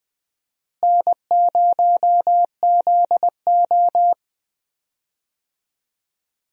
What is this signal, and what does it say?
Morse code, heard as a keyed tone, about 20 wpm, 700 Hz. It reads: N0ZO
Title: USA Call Signs - 4 Characters - Repeated 3x - 20wpm